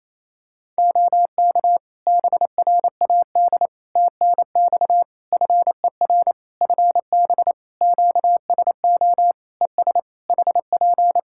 Morse code, heard as a keyed tone, about 27 wpm, 700 Hz. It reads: OK BRAD TNX FER FB QSO ES HP
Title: K1USN SST QSO Practice 28wpm